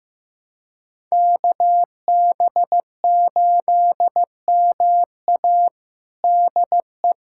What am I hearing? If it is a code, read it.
Morse code, heard as a keyed tone, about 15 wpm, 700 Hz. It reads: KB8MA DE
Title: USA Callsign DE Callsign - 5 Characters - 15wpm